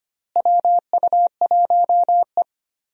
Morse code, heard as a keyed tone, about 25 wpm, 700 Hz. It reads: WU1E
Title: USA Call Signs - 4 Characters - Repeated 3x - 25wpm